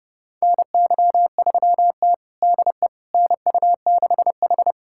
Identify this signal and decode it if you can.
Morse code, heard as a keyed tone, about 31 wpm, 700 Hz. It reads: NY3T DE NU6H
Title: USA Callsign DE Callsign - 4 Characters - 30wpm